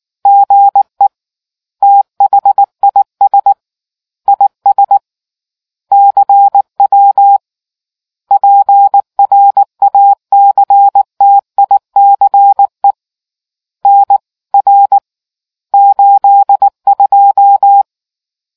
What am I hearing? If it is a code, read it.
Morse code, heard as a keyed tone, about 19 wpm, 800 Hz. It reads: GE THIS IS CW PRACTICE N R 82